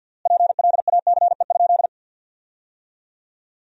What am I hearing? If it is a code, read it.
Morse code, heard as a keyed tone, about 50 wpm, 700 Hz. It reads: PLACE?
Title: 3x Word-Spacing - Sentences from Top 200 Words - 50wpm